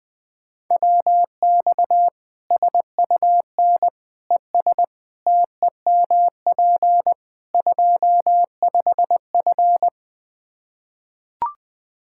Morse code, heard as a keyed tone, about 21 wpm, 700 Hz. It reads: WX SUN ES TEMP 25F
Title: Practice QSO - Exchange 2 - Info - 20wpm